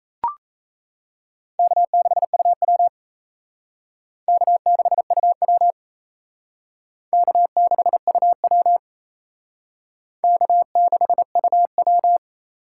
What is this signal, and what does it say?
Morse code, heard as a keyed tone, about 35 wpm, 700 Hz. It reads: K6UW K6UW K6UW K6UW
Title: Speed Racing - USA Call Signs 28wpm